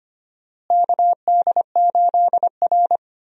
Morse code, heard as a keyed tone, about 25 wpm, 700 Hz. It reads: KD8R
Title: USA Call Signs - 4 Characters - Repeated 3x - 25wpm